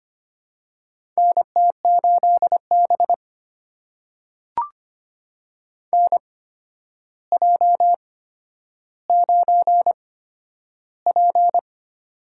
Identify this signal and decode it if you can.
Morse code, heard as a keyed tone, about 25 wpm, spaced as for 11 wpm, 700 Hz. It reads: NT8B NJ9P
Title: USA Call Signs - 4 Characters - 8x Character Spacing - 25wpm